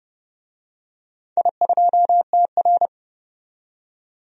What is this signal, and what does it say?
Morse code, heard as a keyed tone, about 29 wpm, 700 Hz. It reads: I2TR
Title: DXCC-100 Call Signs - 4 Characters - 30wpm